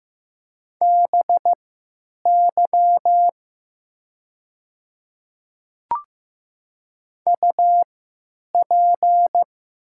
Morse code, heard as a keyed tone, about 15 wpm, spaced as for 9 wpm, 700 Hz. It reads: BY UP
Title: 3x Character Spacing - Top 200 Words - 15wpm